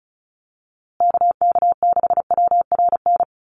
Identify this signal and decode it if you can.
Morse code, heard as a keyed tone, about 34 wpm, 700 Hz. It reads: KK6WRN
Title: USA Call Signs Repeated 2x 35wpm